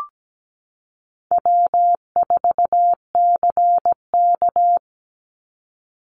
Morse code, heard as a keyed tone, about 17 wpm, 700 Hz. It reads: W4CK
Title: Easy Call Signs - 17wpm